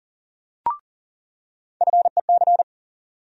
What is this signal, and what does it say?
Morse code, heard as a keyed tone, about 40 wpm, 700 Hz. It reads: FEC
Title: USA Call Sign Postfixes - 3 Letters 40wpm